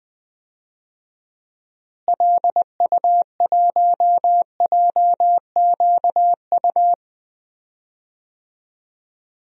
Morse code, heard as a keyed tone, about 20 wpm, 700 Hz. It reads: LU1JQU
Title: DXCC-100 Call Signs - 6 Characters - Repeated 3x - 20wpm